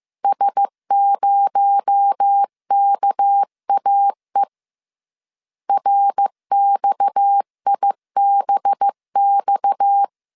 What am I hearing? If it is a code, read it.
Morse code, heard as a keyed tone, about 15 wpm, 800 Hz. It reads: S0KAE RXIBX